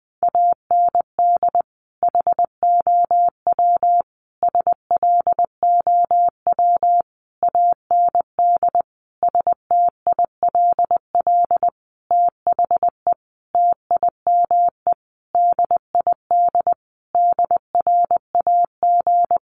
Morse code, heard as a keyed tone, about 20 wpm, 700 Hz. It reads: AND HOW SLOW AND STILL THE TIME DID DRAG
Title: Sentences from Top 1500 Words 20wpm